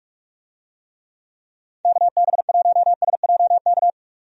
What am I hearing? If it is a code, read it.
Morse code, heard as a keyed tone, about 46 wpm, 700 Hz. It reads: KB1SJK
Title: USA Call Signs - 6 Characters - 45wpm